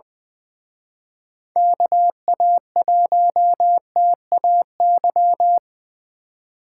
Morse code, heard as a keyed tone, about 24 wpm, 700 Hz. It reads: KA1TAY
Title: Speed Racing - USA 2x3 Call Signs 20wpm